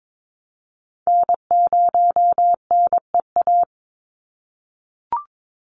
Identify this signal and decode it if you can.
Morse code, heard as a keyed tone, about 22 wpm, 700 Hz. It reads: N0NEA
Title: USA Call Signs 22wpm